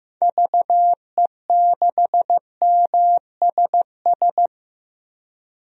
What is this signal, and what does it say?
Morse code, heard as a keyed tone, about 15 wpm, 700 Hz. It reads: VE6MSS
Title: DXCC-100 Call Signs - 6 Characters - 15wpm